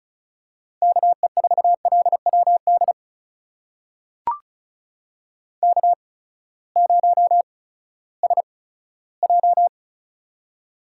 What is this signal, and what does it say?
Morse code, heard as a keyed tone, about 36 wpm, spaced as for 18 wpm, 700 Hz. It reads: KE4LWD K0SJ
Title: USA Call Signs - 8x Character Spacing - 35wpm